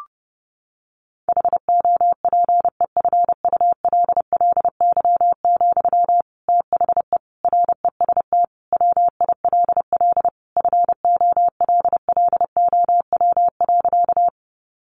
Morse code, heard as a keyed tone, about 30 wpm, 700 Hz. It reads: HOPEFULLY, THE REST WILL FOLLOW.
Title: Sentences from Top 1500 Words 30wpm